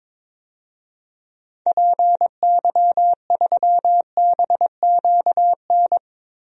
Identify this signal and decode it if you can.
Morse code, heard as a keyed tone, about 22 wpm, 700 Hz. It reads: PY3BQN
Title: DXCC-100 Call Signs - 6 Characters - Repeated 2x - 22wpm